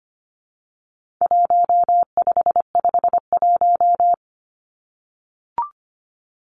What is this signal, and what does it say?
Morse code, heard as a keyed tone, about 25 wpm, 700 Hz. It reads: 1551
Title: Numbers - Contest Style - 4-Digit 25wpm